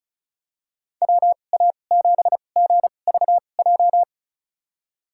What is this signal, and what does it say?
Morse code, heard as a keyed tone, about 34 wpm, spaced as for 25 wpm, 700 Hz. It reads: WA7GVJ
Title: USA Call Signs - 6 Characters - 2x Character Spacing - 35wpm